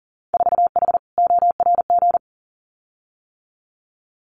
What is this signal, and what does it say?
Morse code, heard as a keyed tone, about 41 wpm, 700 Hz. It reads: 4H ORG
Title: Winter Field Day Exchanges Encore 40wpm